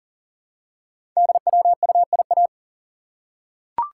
Morse code, heard as a keyed tone, about 40 wpm, 700 Hz. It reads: DWUIA
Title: Code Groups - 5 Letter:Numbers - 40wpm